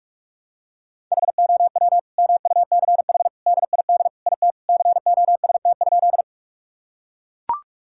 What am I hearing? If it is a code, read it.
Morse code, heard as a keyed tone, about 42 wpm, 700 Hz. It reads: HOW MUCH DID IT COST?
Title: Sentences from Top 1000 Words - Review Part 5 - 45wpm